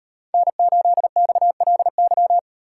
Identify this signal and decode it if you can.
Morse code, heard as a keyed tone, about 38 wpm, 700 Hz. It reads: N8XLY